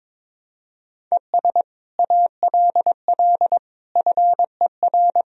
Morse code, heard as a keyed tone, about 22 wpm, 700 Hz. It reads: ES ALL FER
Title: Sentences from Top 100 Words and Abbreviations 22wpm